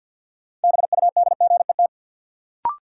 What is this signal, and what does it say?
Morse code, heard as a keyed tone, about 52 wpm, 700 Hz. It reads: BUDGET
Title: Sentences from Top 1200 Words - New Word Drill - 50wpm